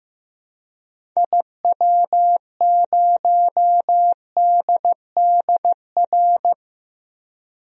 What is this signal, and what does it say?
Morse code, heard as a keyed tone, about 15 wpm, 700 Hz. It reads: IW0DDR
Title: DXCC-100 Call Signs - 6 Characters - Repeated 3x - 15wpm